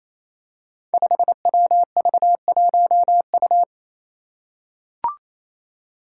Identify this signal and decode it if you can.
Morse code, heard as a keyed tone, about 28 wpm, 700 Hz. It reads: 5WV1U
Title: Five Challenging Letter-Number Combinations 28wpm